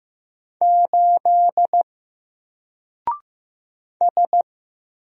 Morse code, heard as a keyed tone, about 15 wpm, 700 Hz. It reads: 8 S
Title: Single Letter-Number Alt 15wpm